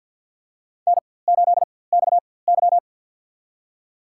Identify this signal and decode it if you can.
Morse code, heard as a keyed tone, about 52 wpm, spaced as for 23 wpm, 700 Hz. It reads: N8XY
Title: USA Call Signs - 4x Character Spacing - 50wpm